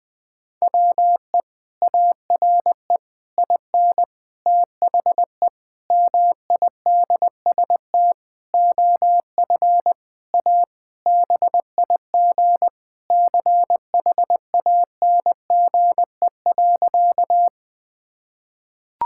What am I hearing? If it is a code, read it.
Morse code, heard as a keyed tone, about 20 wpm, 700 Hz. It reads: WE ARE IN THE MIDST OF A BIG CHANGE.
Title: Sentences with 7-Word Phrases - Spoken Beginning - 20wpm